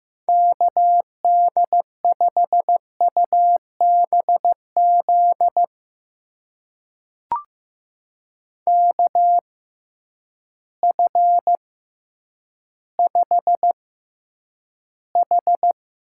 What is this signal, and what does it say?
Morse code, heard as a keyed tone, about 15 wpm, spaced as for 9 wpm, 700 Hz. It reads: KD5UBZ KF5H
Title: USA Call Signs - 6 Characters - 6x Character Spacing - 15wpm